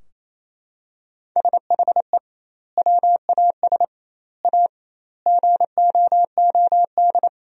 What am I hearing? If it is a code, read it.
Morse code, heard as a keyed tone, about 28 wpm, 700 Hz. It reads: SHE WAS A GOOD
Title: 2x Word-Spacing - Sentences from Top 400 Words - 28wpm